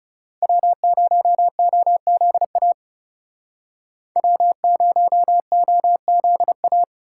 Morse code, heard as a keyed tone, about 37 wpm, 700 Hz. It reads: W0OZA W0OZA
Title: Speed Racing - USA 1x3 Call Signs 30wpm